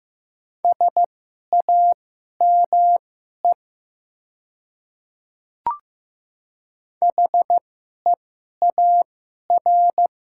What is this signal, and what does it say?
Morse code, heard as a keyed tone, about 15 wpm, spaced as for 11 wpm, 700 Hz. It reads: SAME HEAR
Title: 2x Character Spacing - Top 300 Words - 15wpm